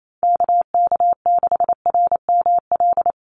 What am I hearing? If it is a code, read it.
Morse code, heard as a keyed tone, about 28 wpm, 700 Hz. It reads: KK6RML
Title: USA Call Signs 28wpm